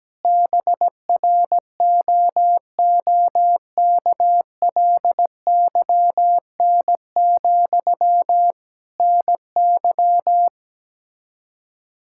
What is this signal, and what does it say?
Morse code, heard as a keyed tone, about 17 wpm, 700 Hz. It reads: BROOKLYN, NY
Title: Practice QSO - Exchange 1 - Beginning - 17wpm